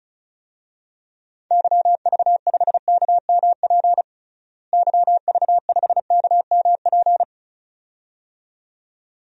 Morse code, heard as a keyed tone, about 36 wpm, 700 Hz. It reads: YV5KMP YV5KMP
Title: DXCC-100 Call Signs - 6 Characters - Repeated 2x - 35wpm